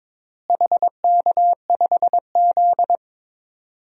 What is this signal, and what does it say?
Morse code, heard as a keyed tone, about 22 wpm, 700 Hz. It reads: HK5Z
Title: DXCC-100 Call Signs - 4 Characters - Repeated 2x - 22wpm